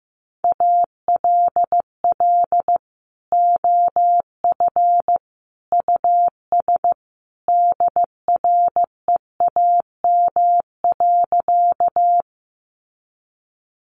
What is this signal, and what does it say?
Morse code, heard as a keyed tone, about 15 wpm, 700 Hz. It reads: ALL OF US DREAM.
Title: Sentences from Top 1500 Words 15wpm